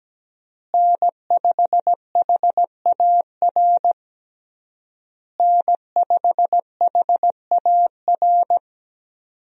Morse code, heard as a keyed tone, about 17 wpm, 700 Hz. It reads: N5HAR N5HAR
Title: USA Call Signs - 5 Characters - Repeated 3x - 17wpm